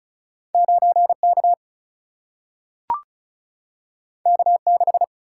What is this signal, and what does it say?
Morse code, heard as a keyed tone, about 34 wpm, 700 Hz. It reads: 9K K6
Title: Single-Letter Single-Number 35wpm